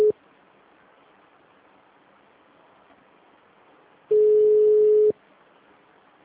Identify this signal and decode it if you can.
Ringback tone (Europe, 425 Hz).